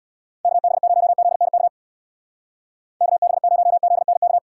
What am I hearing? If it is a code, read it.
Morse code, heard as a keyed tone, about 60 wpm, 700 Hz. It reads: KB0ZNK KB0ZNK
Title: Speed Racing - USA 2x3 Call Signs 50wpm